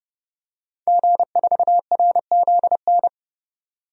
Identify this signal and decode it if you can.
Morse code, heard as a keyed tone, about 32 wpm, 700 Hz. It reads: G4RZN
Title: DXCC-100 Call Signs - 5 Characters - 30wpm